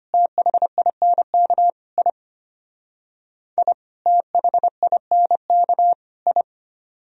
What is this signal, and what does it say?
Morse code, heard as a keyed tone, about 32 wpm, 700 Hz. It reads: THINK I I THINK I